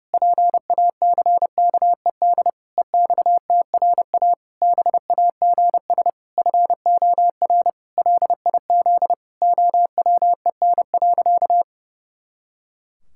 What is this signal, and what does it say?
Morse code, heard as a keyed tone, about 28 wpm, 700 Hz. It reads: PACKED EXTRA BAGS FOR LIZ OWEN.
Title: Pangram Challenge v2 - 30wpm